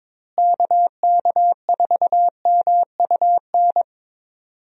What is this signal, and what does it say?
Morse code, heard as a keyed tone, about 22 wpm, 700 Hz. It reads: KK4MUN
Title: USA Call Signs - 6 Characters - Repeated 2x - 22wpm